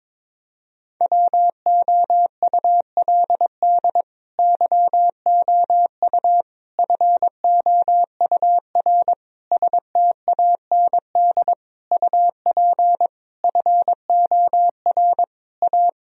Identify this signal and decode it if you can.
Morse code, heard as a keyed tone, about 22 wpm, 700 Hz. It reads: WOULD YOU FOUR STAND UP FOR A
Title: Sentences from Top 1000 Words - Review Part 1 - 22wpm